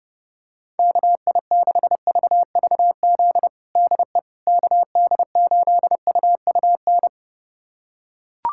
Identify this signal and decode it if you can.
Morse code, heard as a keyed tone, about 30 wpm, 700 Hz. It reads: KI6VVZ DE KD8UUN
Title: USA Callsign DE Callsign - 6 Characters - 30wpm